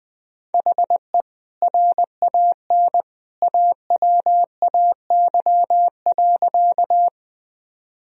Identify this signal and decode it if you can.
Morse code, heard as a keyed tone, about 20 wpm, 700 Hz. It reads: HE RAN AWAY.
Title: Sentences from Top 1000 Words - Review Part 3 - 20wpm